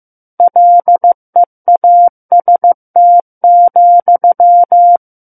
Morse code, heard as a keyed tone, about 15 wpm, 700 Hz. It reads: LEAST,
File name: Chapter 6 - 15 WPM